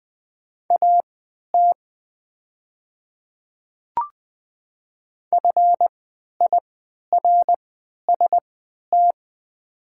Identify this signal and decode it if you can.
Morse code, heard as a keyed tone, about 20 wpm, spaced as for 11 wpm, 690 Hz. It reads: AT FIRST
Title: 3x Character Spacing - Top 200 Words - 20wpm